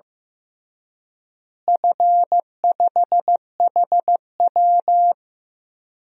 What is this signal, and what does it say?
Morse code, heard as a keyed tone, about 15 wpm, 700 Hz. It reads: F5HW
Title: DXCC-100 Call Signs - 4 Characters - Repeated 3x - 15wpm